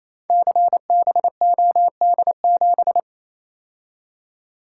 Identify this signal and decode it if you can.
Morse code, heard as a keyed tone, about 27 wpm, 700 Hz. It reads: CBOD7
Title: Code Groups - 5 Letter:Numbers - 28wpm